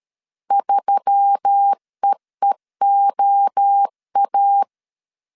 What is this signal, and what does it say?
Morse code, heard as a keyed tone, about 13 wpm, 800 Hz. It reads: 3EEOA
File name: CW / DET / 1 6 1500C S A 12wpm 3440